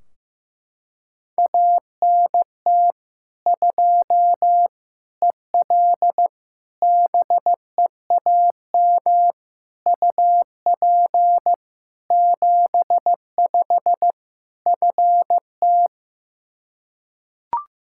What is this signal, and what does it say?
Morse code, heard as a keyed tone, about 15 wpm, 700 Hz. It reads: ANT 2 EL BEAM UP 75 FT
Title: Practice QSO - Exchange 2 - Info - 15wpm